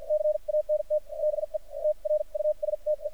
Morse code, heard as a keyed tone, about 57 wpm, spaced as for 43 wpm, 610 Hz. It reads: CANT BE TRUSTE
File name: archived / CRSnet